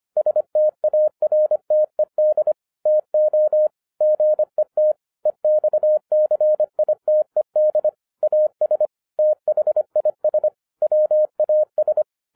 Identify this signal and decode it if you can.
Morse code, heard as a keyed tone, about 25 wpm, 610 Hz. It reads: STARTED TO GET EXCITED AS THIS WAS
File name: 2012-01-mp3 / n06-FIRST FISTS CONTACT-25wpm